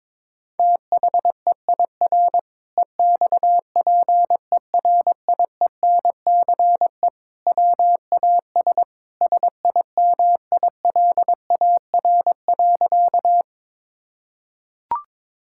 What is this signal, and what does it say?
Morse code, heard as a keyed tone, about 22 wpm, 700 Hz. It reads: THEIR EXPERIENCE WAS SIMILAR.
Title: Sentences from Top 1000 Words - Review Part 5 - 22wpm